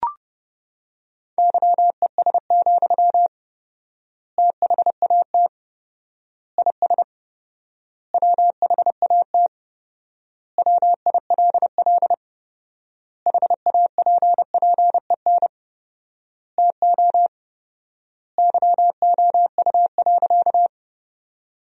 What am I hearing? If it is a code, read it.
Morse code, heard as a keyed tone, about 30 wpm, 700 Hz. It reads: YES, THAT IS WHAT WILL HAPPEN TO YOU.
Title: 4x Word-Spacing - Sentences from Top 500 Words - 30wpm